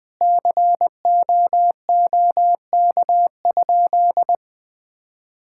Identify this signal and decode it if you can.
Morse code, heard as a keyed tone, about 20 wpm, 700 Hz. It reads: COOK?
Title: Sentences with 7-Word Phrases - Spoken Beginning - 20wpm